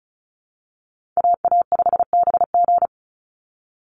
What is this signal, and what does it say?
Morse code, heard as a keyed tone, about 34 wpm, 700 Hz. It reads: AA5BG
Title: Call Signs Repeated 3x - 35wpm